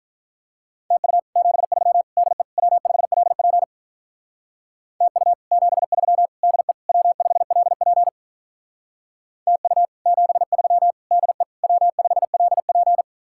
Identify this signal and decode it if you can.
Morse code, heard as a keyed tone, about 43 wpm, 700 Hz. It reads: TU 73 DE W5LP TU 73 DE W5LP TU 73 DE W5LP
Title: Speed-Racing Common QSO Phrases - 35wpm